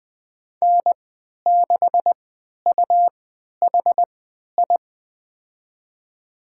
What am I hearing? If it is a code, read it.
Morse code, heard as a keyed tone, about 20 wpm, spaced as for 11 wpm, 700 Hz. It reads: N6UHI
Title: USA Call Signs - 5 Characters - 3x Character Spacing - 20wpm